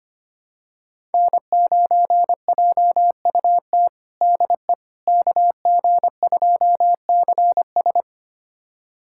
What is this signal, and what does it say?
Morse code, heard as a keyed tone, about 25 wpm, 700 Hz. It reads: N9JUT DE KG2CS
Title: USA Callsign DE Callsign - 5 Characters - 25wpm